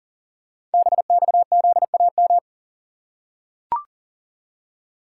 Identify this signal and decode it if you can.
Morse code, heard as a keyed tone, about 40 wpm, 700 Hz. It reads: BXZAM
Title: Code Groups - 5 Letters - 40wpm